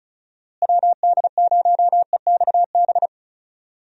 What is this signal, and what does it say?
Morse code, heard as a keyed tone, about 34 wpm, 700 Hz. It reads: WD0EXB
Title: USA Call Signs - 6 Characters - Repeated 2x - 35wpm